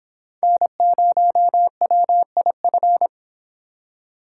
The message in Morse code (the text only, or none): N0WIF